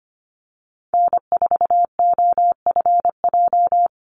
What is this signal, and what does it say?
Morse code, heard as a keyed tone, about 25 wpm, 700 Hz. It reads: N4OFJ
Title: USA Call Signs 25wpm